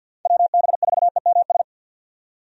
Morse code, heard as a keyed tone, about 48 wpm, 700 Hz. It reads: WB4EMS
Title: USA Call Signs - 6 Characters - 50wpm